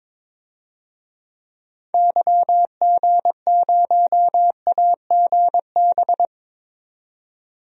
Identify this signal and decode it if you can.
Morse code, heard as a keyed tone, about 22 wpm, 700 Hz. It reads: YG0AGB
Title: DXCC-100 Call Signs - 6 Characters - 22wpm